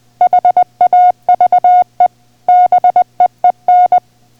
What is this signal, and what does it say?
Morse code, heard as a keyed tone, about 20 wpm, 720 Hz. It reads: HAVE BEEN